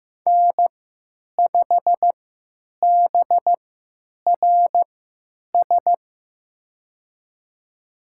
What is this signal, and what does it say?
Morse code, heard as a keyed tone, about 15 wpm, spaced as for 9 wpm, 700 Hz. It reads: N5BRS
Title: USA Call Signs - 3x Character Spacing - 15wpm